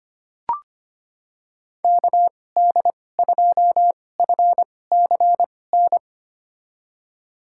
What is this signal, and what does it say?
Morse code, heard as a keyed tone, about 25 wpm, spaced as for 18 wpm, 700 Hz. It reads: KD2FCN
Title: USA Call Signs - 6 Characters - 2x Character Spacing - 25wpm